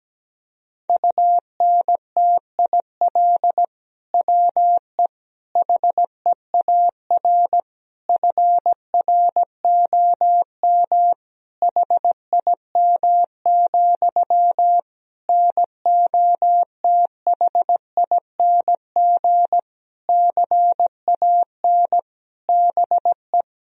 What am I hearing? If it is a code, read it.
Morse code, heard as a keyed tone, about 17 wpm, 700 Hz. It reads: UNTIL WE HEAR FROM HIM, NOTHING CAN BE
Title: Sentences from Top 1000 Words - Review Part 1 - 17wpm